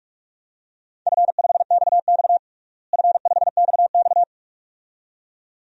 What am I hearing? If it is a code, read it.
Morse code, heard as a keyed tone, about 46 wpm, 700 Hz. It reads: F5XX F5XX
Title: DXCC-100 Call Signs - 4 Characters - Repeated 2x - 45wpm